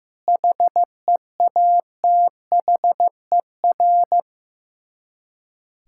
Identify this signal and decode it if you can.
Morse code, heard as a keyed tone, about 15 wpm, 700 Hz. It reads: HEATHER
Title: K1USN SST QSO Practice 15wpm